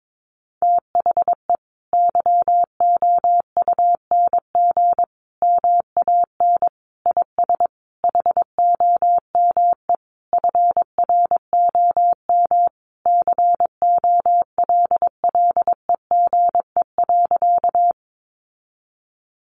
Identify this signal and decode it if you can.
Morse code, heard as a keyed tone, about 22 wpm, 700 Hz. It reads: THE YOUNG MAN IS HOME FROM COLLEGE.
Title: Sentences from Top 1300 Words 22wpm